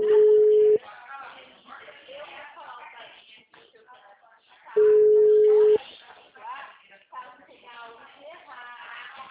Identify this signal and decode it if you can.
Ringback tone (Europe, 425 Hz), with speech.